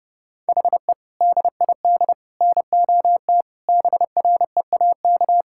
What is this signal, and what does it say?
Morse code, heard as a keyed tone, about 30 wpm, 700 Hz. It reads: HE DID NOT BREAK
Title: Sentences from Top 1000 Words - Review Part 6 - 30wpm